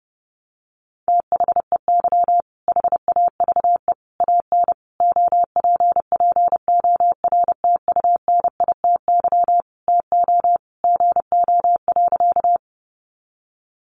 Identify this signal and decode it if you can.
Morse code, heard as a keyed tone, about 30 wpm, 700 Hz. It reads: THEY HAVE AN OPPORTUNITY TO GO.
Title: Sentences from Top 1000 Words 30wpm